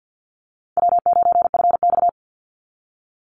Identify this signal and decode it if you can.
Morse code, heard as a keyed tone, about 50 wpm, 700 Hz. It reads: W9FX
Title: Easy Call Signs Repeated 3x - 50wpm